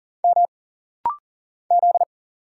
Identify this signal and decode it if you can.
Morse code, heard as a keyed tone, about 39 wpm, 700 Hz. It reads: M Z
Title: Single Letters - Rapid-Fire 40wpm